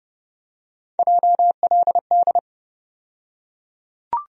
Morse code, heard as a keyed tone, about 30 wpm, 700 Hz. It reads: JLD
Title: USA Call Sign Postfixes - 3 Letters 30wpm